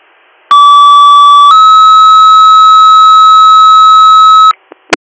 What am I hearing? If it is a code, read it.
Two-tone page, heard as a one-second tone, then a three-second tone, about 1150 then 1290 Hz. It is heard from 0.5 to 4.5 s.